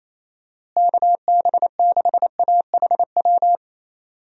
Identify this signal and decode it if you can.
Morse code, heard as a keyed tone, about 27 wpm, 700 Hz. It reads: KB6AHW
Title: USA Call Signs - 6 Characters - Repeated 3x - 28wpm